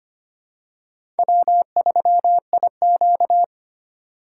Morse code, heard as a keyed tone, about 25 wpm, 700 Hz. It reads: W3IQ
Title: USA Call Signs - 4x Character Spacing - 25wpm